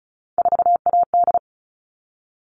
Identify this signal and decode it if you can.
Morse code, heard as a keyed tone, about 34 wpm, 700 Hz. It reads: 4AD
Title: All Cut Numbers - 3-Digit 35wpm